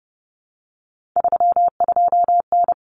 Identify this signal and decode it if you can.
Morse code, heard as a keyed tone, about 30 wpm, 700 Hz. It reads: 32N
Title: Common Cut Numbers - 3-Digit 30wpm